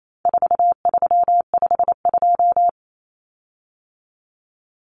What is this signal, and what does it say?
Morse code, heard as a keyed tone, about 28 wpm, 700 Hz. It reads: 4352
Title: Common Cut Numbers - 4-Digit 28wpm